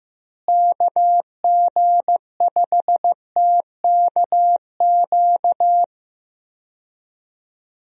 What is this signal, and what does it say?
Morse code, heard as a keyed tone, about 16 wpm, 700 Hz. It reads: KG5TKQ
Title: USA Call Signs - 6 Characters - Repeated 2x - 15wpm